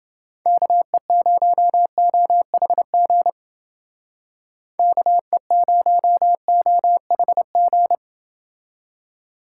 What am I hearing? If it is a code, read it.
Morse code, heard as a keyed tone, about 28 wpm, 700 Hz. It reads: KE0OHG KE0OHG